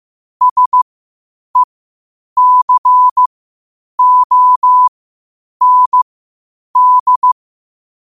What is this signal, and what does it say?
Morse code, heard as a keyed tone, about 15 wpm, spaced as for 8 wpm, 1000 Hz. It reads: SECOND
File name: qotdfiles / 8wpm